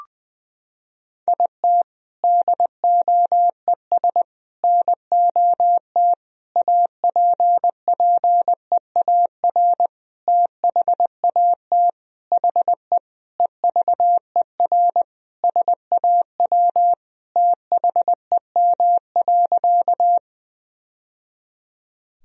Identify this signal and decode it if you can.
Morse code, heard as a keyed tone, about 20 wpm, 700 Hz. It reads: IT DOES NOT APPEAR THAT HE EVER SAW THEM.
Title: Sentences from Top 1000 Words - Review Part 2 - 20wpm